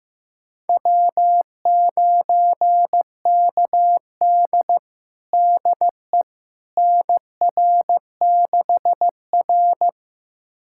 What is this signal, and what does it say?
Morse code, heard as a keyed tone, about 15 wpm, 700 Hz. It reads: W9KD DE NR6R
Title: USA Callsign DE Callsign - 4 Characters - 15wpm